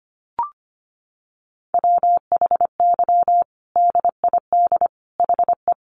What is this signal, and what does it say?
Morse code, heard as a keyed tone, about 25 wpm, 700 Hz. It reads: WHY DID HE
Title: Sentences from Top 200 Words 25wpm